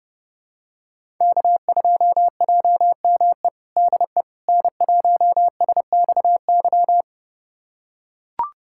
Morse code, heard as a keyed tone, about 32 wpm, 700 Hz. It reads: K2JME DE N1SXY
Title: USA Callsign DE Callsign - 5 Characters - 30wpm